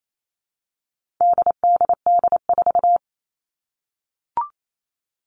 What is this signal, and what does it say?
Morse code, heard as a keyed tone, about 28 wpm, 700 Hz. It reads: DDD4
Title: All Cut Numbers - 4-Digit 28wpm